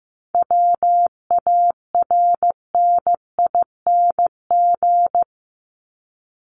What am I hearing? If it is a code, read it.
Morse code, heard as a keyed tone, about 15 wpm, 700 Hz. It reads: WARNING
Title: Top 1751-2000 Words 15wpm